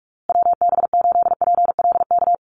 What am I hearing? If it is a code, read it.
Morse code, heard as a keyed tone, about 44 wpm, 700 Hz. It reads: WB8PLK
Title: USA Call Signs Repeated 2x 45wpm